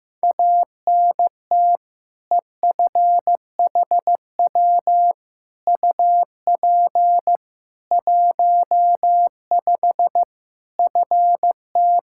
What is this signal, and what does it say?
Morse code, heard as a keyed tone, about 15 wpm, 700 Hz. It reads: ANT EFHW UP 15 FT
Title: Practice QSO - Exchange 2 - Info - 15wpm